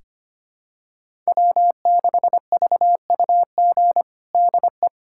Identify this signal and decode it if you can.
Morse code, heard as a keyed tone, about 25 wpm, 700 Hz. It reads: W6VUG DE
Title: USA Callsign DE Callsign - 5 Characters - 25wpm